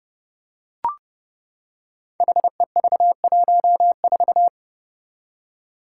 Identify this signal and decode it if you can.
Morse code, heard as a keyed tone, about 32 wpm, 700 Hz. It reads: HEV14
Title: Five Challenging Letter-Number Combinations 30wpm